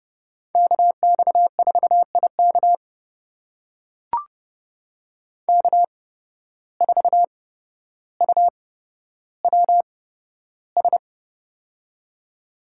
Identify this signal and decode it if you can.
Morse code, heard as a keyed tone, about 30 wpm, spaced as for 13 wpm, 700 Hz. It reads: KX4IK K4UWS
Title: USA Call Signs - 5 Characters - 8x Character Spacing - 30wpm